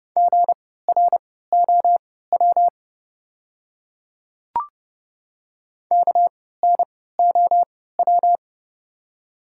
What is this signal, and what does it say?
Morse code, heard as a keyed tone, about 30 wpm, spaced as for 17 wpm, 700 Hz. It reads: GROW KNOW
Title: 3x Character Spacing - Top 300 Words - 30wpm